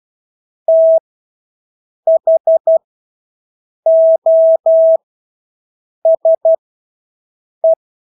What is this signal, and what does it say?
Morse code, heard as a keyed tone, about 12 wpm, spaced as for 6 wpm, 650 Hz. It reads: THOSE
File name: Engelske ord 12-6 wpm